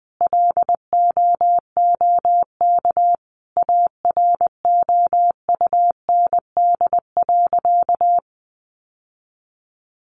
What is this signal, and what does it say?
Morse code, heard as a keyed tone, about 20 wpm, 700 Hz. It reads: LOOK AROUND.
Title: Sentences from Top 800 Words 20wpm